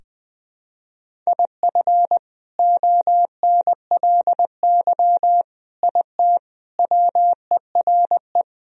Morse code, heard as a keyed tone, about 20 wpm, 700 Hz. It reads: IF ONLY IT WERE
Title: Sentences from Top 500 Words and Abbreviations 20wpm